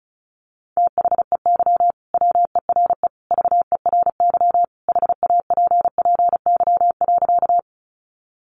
Morse code, heard as a keyed tone, about 34 wpm, 700 Hz. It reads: THEY WERE VERY HAPPY.
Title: Sentences from Top 1000 Words 35wpm